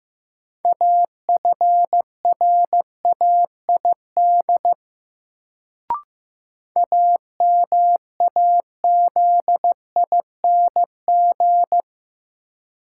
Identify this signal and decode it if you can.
Morse code, heard as a keyed tone, about 15 wpm, 700 Hz. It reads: AFRAID AMAZING
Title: Sentences from Top 1500 Words - New Word Drill - 15wpm